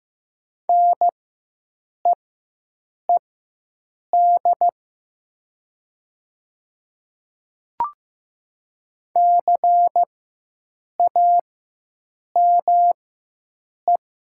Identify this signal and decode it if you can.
Morse code, heard as a keyed tone, about 15 wpm, spaced as for 7 wpm, 700 Hz. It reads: NEED CAME
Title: 4x Character Spacing - Top 300 Words - 15wpm